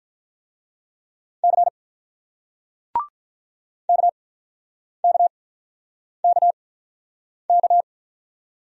Morse code, heard as a keyed tone, about 48 wpm, spaced as for 9 wpm, 700 Hz. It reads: C KKKK